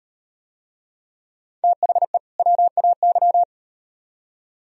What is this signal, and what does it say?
Morse code, heard as a keyed tone, about 38 wpm, 700 Hz. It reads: THE WAY